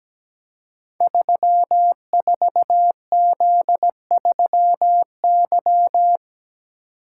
Morse code, heard as a keyed tone, about 17 wpm, 700 Hz. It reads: 34Z3Y
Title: Code Groups - 5 Letter:Numbers - 17wpm